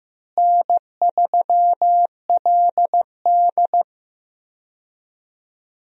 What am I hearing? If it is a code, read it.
Morse code, heard as a keyed tone, about 15 wpm, 700 Hz. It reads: N3LD
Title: USA Call Signs - 4 Characters - Repeated 2x - 15wpm